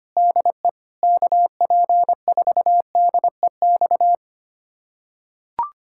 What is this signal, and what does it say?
Morse code, heard as a keyed tone, about 25 wpm, 700 Hz. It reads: DE KP4DEX
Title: USA Callsign DE Callsign - 6 Characters - 25wpm